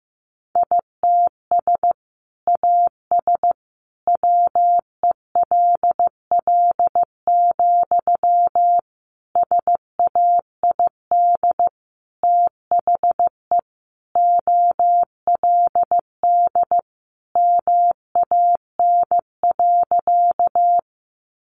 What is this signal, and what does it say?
Morse code, heard as a keyed tone, about 15 wpm, 700 Hz. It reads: ITS AS WELL, SAID THE OLD MAN.
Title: Sentences from Top 600 Words 15wpm